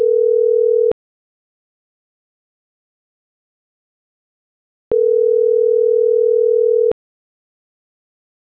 Ringback tone (North America), 440 + 480 Hz.